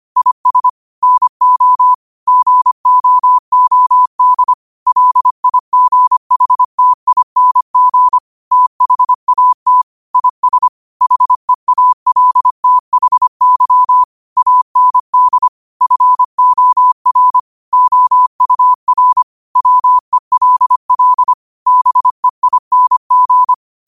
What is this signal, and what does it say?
Morse code, heard as a keyed tone, about 25 wpm, 1000 Hz. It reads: IS NO GOOD LIGHTING THAT IS HEALTHY AND FOR OUR WELL BEING
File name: 25wpm